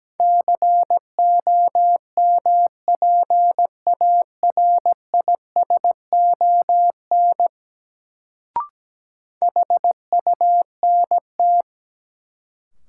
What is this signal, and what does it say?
Morse code, heard as a keyed tone, about 17 wpm, 700 Hz. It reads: COMPARISON HUNT
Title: Sentences from Top 1600 Words - New Word Drill - 17wpm